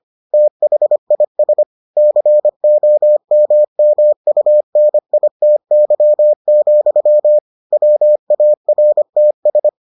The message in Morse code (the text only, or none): THIS COMMUNITY, WARTS